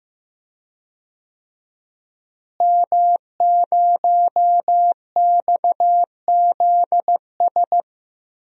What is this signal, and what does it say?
Morse code, heard as a keyed tone, about 15 wpm, 700 Hz. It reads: M0XZS
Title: DXCC-100 Call Signs - 5 Characters - Repeated 2x - 15wpm